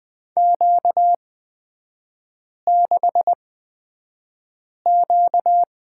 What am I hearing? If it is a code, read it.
Morse code, heard as a keyed tone, about 20 wpm, spaced as for 5 wpm, 690 Hz. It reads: Q6Q
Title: 236 Lesson 38e - Letter q - Three Characters 20wpm